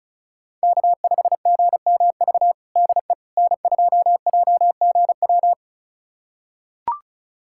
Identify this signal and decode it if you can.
Morse code, heard as a keyed tone, about 36 wpm, 700 Hz. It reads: K5GMV DE N2JGW
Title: USA Callsign DE Callsign - 5 Characters - 35wpm